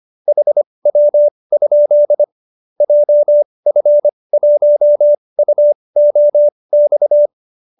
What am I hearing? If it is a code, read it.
Morse code, heard as a keyed tone, about 25 wpm, spaced as for 20 wpm, 590 Hz. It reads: HW? JF1UOX
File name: qso216 20